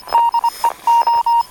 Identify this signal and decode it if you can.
Morse code, heard as a keyed tone, about 25 wpm, 930 Hz, with radio noise. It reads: DE O
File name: attachments / 19.04.2017 OK1FIM RX DSB